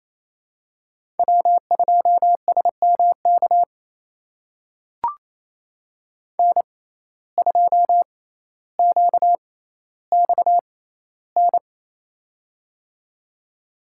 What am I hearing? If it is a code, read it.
Morse code, heard as a keyed tone, about 28 wpm, spaced as for 14 wpm, 700 Hz. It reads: W2SMK N2QXN
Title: USA Call Signs - 5 Characters - 6x Character Spacing - 28wpm